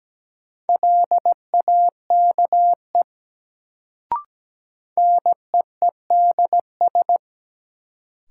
Morse code, heard as a keyed tone, about 17 wpm, 700 Hz. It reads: LAKE NEEDS
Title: Sentences from Top 1000 Words - Review Part 3 - New Word Drill - 17wpm